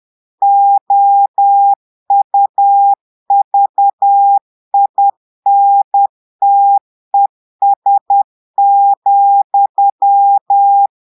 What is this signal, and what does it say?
Morse code, heard as a keyed tone, about 10 wpm, 800 Hz. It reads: OUVINTES,
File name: portugais texte en clair 2 vitesse 10 mots minutes